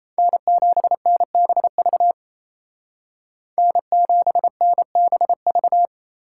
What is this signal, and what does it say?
Morse code, heard as a keyed tone, about 33 wpm, 700 Hz. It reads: N7NBV N7NBV
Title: Speed Racing - USA 1x3 Call Signs 28wpm